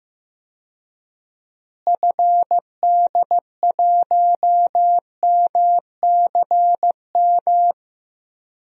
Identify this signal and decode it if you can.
Morse code, heard as a keyed tone, about 15 wpm, 700 Hz. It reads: FD1MCM
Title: DXCC-100 Call Signs - 6 Characters - Repeated 3x - 15wpm